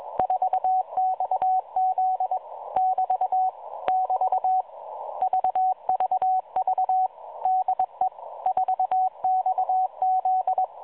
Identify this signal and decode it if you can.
Morse code, heard as a keyed tone, about 22 wpm, 750 Hz, with radio noise. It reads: 4XZ <BT> <BT> VVV DE 4XZ
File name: 4XZ Shazepe